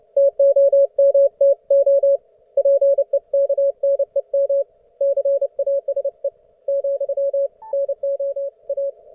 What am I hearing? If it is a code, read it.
Morse code, heard as a keyed tone, about 31 wpm, 560 Hz. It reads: TOMTO PEKNEM CASE , NOA